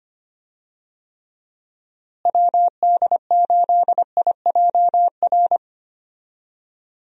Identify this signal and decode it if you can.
Morse code, heard as a keyed tone, about 25 wpm, 700 Hz. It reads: WD8IJR
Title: USA Call Signs - 6 Characters - Repeated 3x - 25wpm